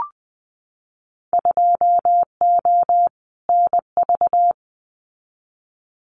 Morse code, heard as a keyed tone, about 19 wpm, 700 Hz. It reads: 2O NV
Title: Winter Field Day Exchanges 20wpm